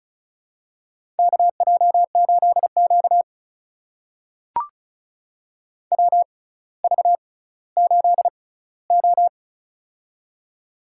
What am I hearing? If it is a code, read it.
Morse code, heard as a keyed tone, about 35 wpm, spaced as for 18 wpm, 700 Hz. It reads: KJ8Q WV8O
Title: USA Call Signs - 4 Characters - 6x Character Spacing - 35wpm